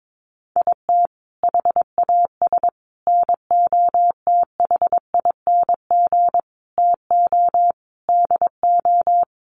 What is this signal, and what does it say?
Morse code, heard as a keyed tone, about 22 wpm, 700 Hz. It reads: IT HAS NOTHING TO DO
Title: Sets of 5 Words Encore 22wpm